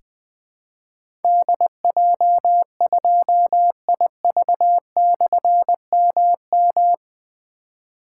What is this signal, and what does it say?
Morse code, heard as a keyed tone, about 20 wpm, 700 Hz. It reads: DJ2IV/MM
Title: Difficult Call Signs 20wpm